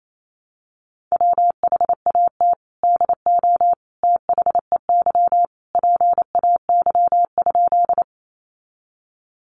Morse code, heard as a keyed tone, about 28 wpm, 700 Hz. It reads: WHAT DO THEY PAY?
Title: Sentences from Top 700 Words 28wpm